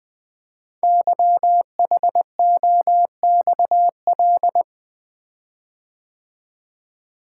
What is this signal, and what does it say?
Morse code, heard as a keyed tone, about 21 wpm, 700 Hz. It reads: YHOXL
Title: Code Groups - 5 Letters - 20wpm